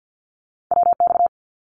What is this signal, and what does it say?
Morse code, heard as a keyed tone, about 48 wpm, 700 Hz. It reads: WX
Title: Abbreviations Repeated 3x - 50wpm